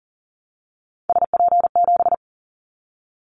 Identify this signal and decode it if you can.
Morse code, heard as a keyed tone, about 40 wpm, 700 Hz. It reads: SP7
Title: Callsign Prefixes for Top 25 DXCC - 40wpm